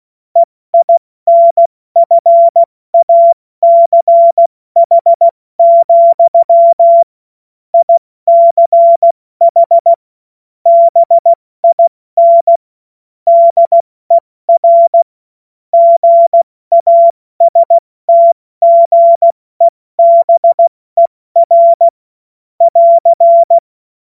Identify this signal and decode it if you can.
Morse code, heard as a keyed tone, about 16 wpm, 680 Hz. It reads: EINFACH, ICH BIN DER GASTGEBER <AR>
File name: Morsekurs / Witze / Witz 08. Februar 2022 70 F80